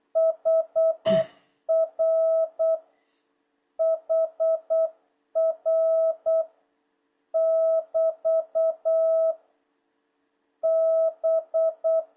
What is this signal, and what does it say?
Morse code, heard as a keyed tone, about 8 wpm, 650 Hz. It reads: HR HR <BT> B
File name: jh8cbh / 2026cw